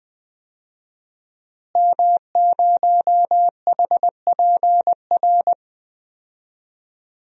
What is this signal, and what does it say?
Morse code, heard as a keyed tone, about 20 wpm, 700 Hz. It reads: M0HPR
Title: DXCC-100 Call Signs - 5 Characters - Repeated 2x - 20wpm